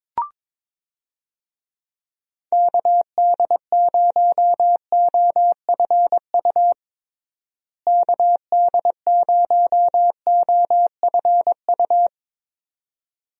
Morse code, heard as a keyed tone, about 21 wpm, 700 Hz. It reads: KD0OFU KD0OFU
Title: USA Call Signs - 6 Characters - Repeated 2x - 22wpm